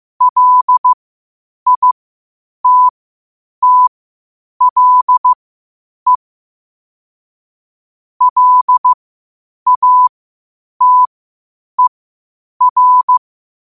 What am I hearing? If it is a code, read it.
Morse code, heard as a keyed tone, about 15 wpm, spaced as for 8 wpm, 1000 Hz. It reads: LITTLE LATER
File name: qotdfiles / 8wpm